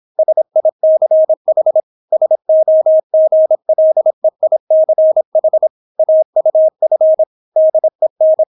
Morse code, heard as a keyed tone, about 26 wpm, 630 Hz. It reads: SICH SOGLEICH AUF DEN